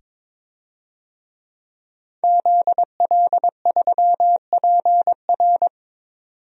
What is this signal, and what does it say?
Morse code, heard as a keyed tone, about 21 wpm, 700 Hz. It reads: ZL3PR
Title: DXCC-100 Call Signs - 5 Characters - 22wpm